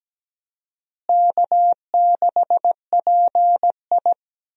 Morse code, heard as a keyed tone, about 17 wpm, 700 Hz. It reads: K6PI
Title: USA Call Signs - 4 Characters - Repeated 2x - 17wpm